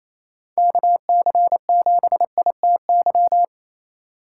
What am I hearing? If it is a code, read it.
Morse code, heard as a keyed tone, about 34 wpm, 700 Hz. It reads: KC7ITY
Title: Speed Racing - USA 2x3 Call Signs 28wpm